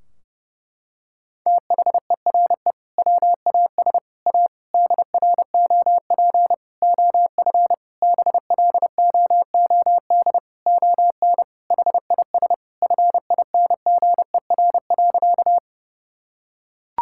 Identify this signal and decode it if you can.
Morse code, heard as a keyed tone, about 30 wpm, 700 Hz. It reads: THERE WAS A DROP OF BLOOD ON HIS FINGER.
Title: Sentences from Top 1600 Words 30wpm